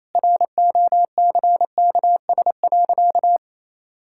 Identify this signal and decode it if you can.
Morse code, heard as a keyed tone, about 27 wpm, 700 Hz. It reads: ROCKS.
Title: Pangram Challenge v2 - 28wpm